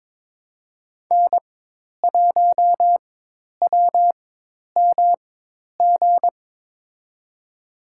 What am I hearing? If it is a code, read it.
Morse code, heard as a keyed tone, about 22 wpm, spaced as for 10 wpm, 700 Hz. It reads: N1WMG
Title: USA Call Signs - 5 Characters - 4x Character Spacing - 22wpm